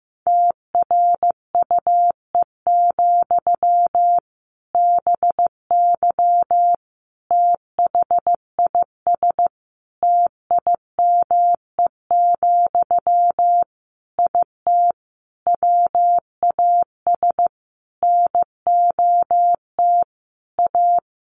Morse code, heard as a keyed tone, about 15 wpm, 700 Hz. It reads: TRUE, BY THIS TIME, IT WAS NOT A
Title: Sentences from Top 1500 Words 15wpm